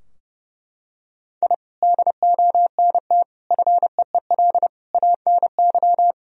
Morse code, heard as a keyed tone, about 30 wpm, 700 Hz. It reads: I DONT FEEL ANY
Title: Sentences from Top 1000 Words - Review Part 3 - 30wpm